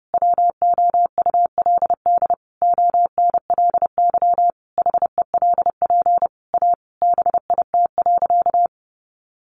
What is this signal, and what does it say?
Morse code, heard as a keyed tone, about 30 wpm, 700 Hz. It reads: WOULD ONLY HELP A BIT.
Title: Sentences from Top 900 Words 30wpm